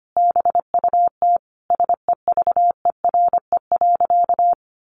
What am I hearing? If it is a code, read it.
Morse code, heard as a keyed tone, about 25 wpm, 700 Hz. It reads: BUT SEVERE.
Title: Sentences from Top 1400 Words 25wpm